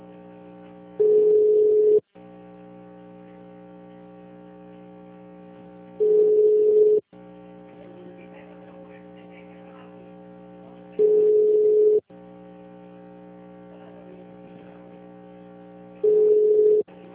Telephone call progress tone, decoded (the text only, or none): ringback tone